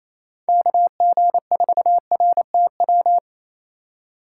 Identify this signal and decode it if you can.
Morse code, heard as a keyed tone, about 28 wpm, 700 Hz. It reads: KG4RTW
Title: USA Call Signs - 6 Characters - 4x Character Spacing - 28wpm